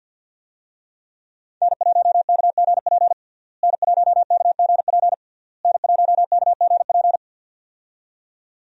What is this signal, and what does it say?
Morse code, heard as a keyed tone, about 46 wpm, 700 Hz. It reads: N1KGP N1KGP N1KGP
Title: USA Call Signs - 5 Characters - Repeated 3x - 50wpm